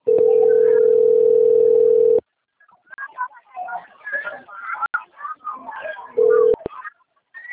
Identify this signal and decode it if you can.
Ringback tone (North America), 440 + 480 Hz.